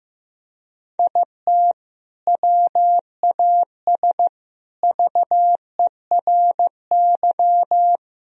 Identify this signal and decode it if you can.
Morse code, heard as a keyed tone, about 15 wpm, 700 Hz. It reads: IT WAS VERY
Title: Sentences from Top 1000 Words - Review Part 1 - 15wpm